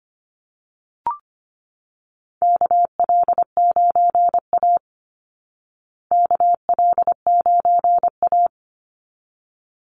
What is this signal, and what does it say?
Morse code, heard as a keyed tone, about 25 wpm, 700 Hz. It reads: KL9A KL9A
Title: Call Signs Repeated 3x - 25wpm